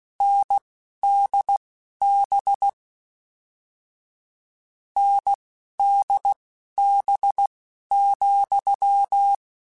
Morse code, heard as a keyed tone, about 16 wpm, spaced as for 11 wpm, 800 Hz. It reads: NDB NDB,